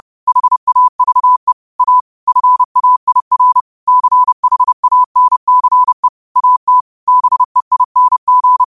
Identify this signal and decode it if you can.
Morse code, heard as a keyed tone, about 30 wpm, 1000 Hz. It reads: HAVE A FAIR CHANCE AT BEING
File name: qotdfiles / 30wpm